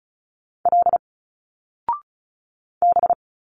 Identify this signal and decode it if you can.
Morse code, heard as a keyed tone, about 34 wpm, 700 Hz. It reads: L B
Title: Single Letters 35wpm